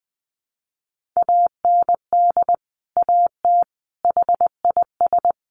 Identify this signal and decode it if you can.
Morse code, heard as a keyed tone, about 20 wpm, 700 Hz. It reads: AND AT HIS
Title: Sentences from Top 1500 Words 20wpm